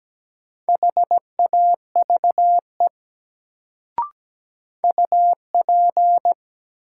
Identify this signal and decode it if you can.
Morse code, heard as a keyed tone, about 17 wpm, 700 Hz. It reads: HAVE UP
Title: Sets of 2 Words - New Word Drill - 17wpm